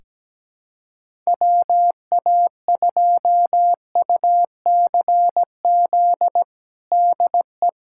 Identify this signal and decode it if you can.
Morse code, heard as a keyed tone, about 17 wpm, 700 Hz. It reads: WA2UCZ DE
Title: USA Callsign DE Callsign - 6 Characters - 17wpm